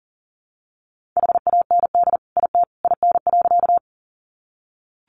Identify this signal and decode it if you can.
Morse code, heard as a keyed tone, about 40 wpm, 700 Hz. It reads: HAND IT IN.
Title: Sentences from Top 200 Words 40wpm